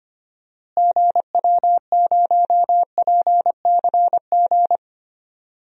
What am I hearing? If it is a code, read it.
Morse code, heard as a keyed tone, about 25 wpm, 700 Hz. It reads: GW0PCG
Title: DXCC-100 Call Signs - 6 Characters - Repeated 3x - 25wpm